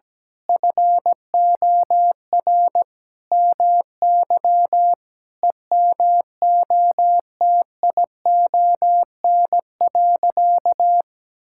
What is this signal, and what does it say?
Morse code, heard as a keyed tone, about 17 wpm, 700 Hz. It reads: FOR MY EMOTION.
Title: Sentences from Top 1600 Words 17wpm